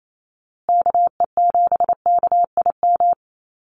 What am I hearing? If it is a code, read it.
Morse code, heard as a keyed tone, about 28 wpm, 700 Hz. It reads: KE7KIM
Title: USA Call Signs Repeated 3x 28wpm